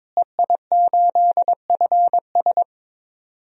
Morse code, heard as a keyed tone, about 22 wpm, 700 Hz. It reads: EI8FS
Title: DXCC-100 Call Signs - 5 Characters - Repeated 3x - 22wpm